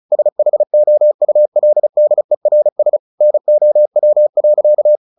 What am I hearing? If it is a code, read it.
Morse code, heard as a keyed tone, about 34 wpm, 600 Hz. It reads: SHOULDERS NOW.
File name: the old man and the sea / the old man and the sea 1 35wpm